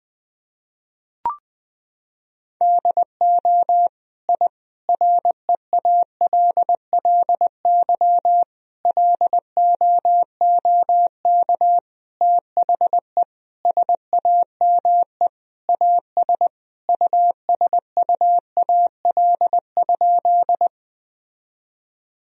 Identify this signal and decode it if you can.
Morse code, heard as a keyed tone, about 20 wpm, 700 Hz. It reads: DO I REALLY LOOK THE SAME AS USUAL?
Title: Sentences from Top 1000 Words - Review Part 2 - 20wpm